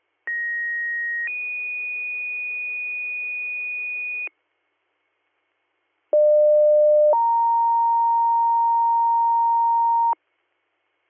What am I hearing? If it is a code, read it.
Two-tone page, heard as a one-second tone, then a three-second tone, about 1900 then 2450 Hz, about 600 then 930 Hz. It is heard from 0.3 to 10.1 s.